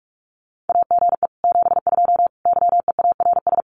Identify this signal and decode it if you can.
Morse code, heard as a keyed tone, about 44 wpm, 700 Hz. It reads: AGE 72 YEARS
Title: Common QSO Phrases Encore 45wpm